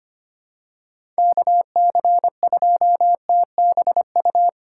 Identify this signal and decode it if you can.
Morse code, heard as a keyed tone, about 25 wpm, 700 Hz. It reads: KC2TBU
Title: USA Call Signs - 6 Characters - 4x Character Spacing - 25wpm